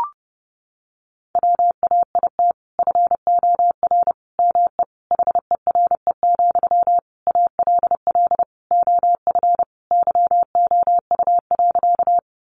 Morse code, heard as a keyed tone, about 30 wpm, 700 Hz. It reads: WAIT FOR ME HERE, ALL OF YOU.
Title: Sentences from Top 500 Words 30wpm